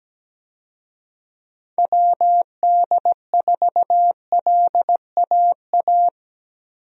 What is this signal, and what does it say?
Morse code, heard as a keyed tone, about 17 wpm, 700 Hz. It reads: WD4LAA
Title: USA Call Signs - 6 Characters - Repeated 3x - 17wpm